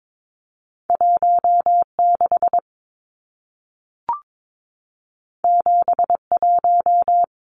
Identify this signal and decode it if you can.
Morse code, heard as a keyed tone, about 22 wpm, 700 Hz. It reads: 16 71
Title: Numbers - 2-Digit 22wpm